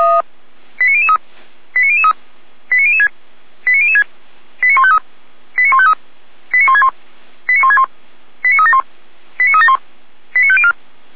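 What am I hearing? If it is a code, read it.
Five-tone selective call (ZVEI), heard as quick runs of five tones, tones of about 70 ms. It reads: 89003 89003 89006 89006 89153 89153 89161 89161 89371 89371 89504
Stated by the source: ZVEI five-tone call 89099 received with an SDR by Feuerwehr Orlamünde.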